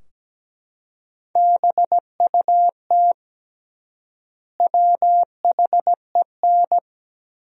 Morse code, heard as a keyed tone, about 17 wpm, 700 Hz. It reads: BUT WHEN
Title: 3x Word-Spacing - Sentences from Top 100 Words - 17wpm